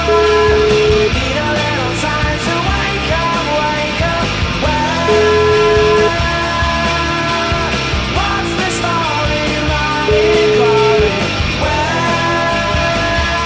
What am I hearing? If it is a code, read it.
Ringback tone (Europe, 425 Hz), with noise.